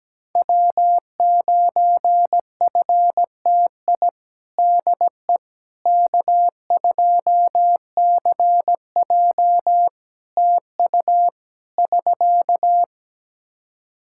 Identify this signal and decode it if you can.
Morse code, heard as a keyed tone, about 17 wpm, 700 Hz. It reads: W9FTI DE K2CJ TU <SK>
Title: K1USN SST QSO Practice 17wpm